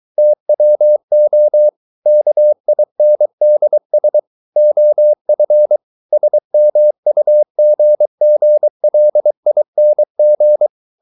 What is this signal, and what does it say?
Morse code, heard as a keyed tone, about 23 wpm, 600 Hz. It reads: TWO KINDS OF SMUGGLING